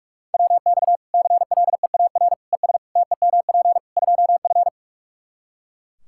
Morse code, heard as a keyed tone, about 41 wpm, 700 Hz. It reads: WX CLEAR ES TEMP 2F
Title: Practice QSO - Exchange 2 - Info - 45wpm